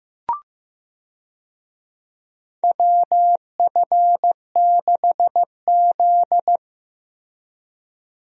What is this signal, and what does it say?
Morse code, heard as a keyed tone, about 15 wpm, 700 Hz. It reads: WF6Z
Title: USA Call Signs - 4 Characters - Repeated 3x - 15wpm